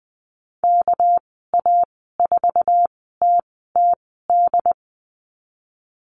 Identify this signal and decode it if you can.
Morse code, heard as a keyed tone, about 20 wpm, spaced as for 14 wpm, 700 Hz. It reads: KA4TTD
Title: 213 Lesson 34g - Number 8 - Call Signs 20wpm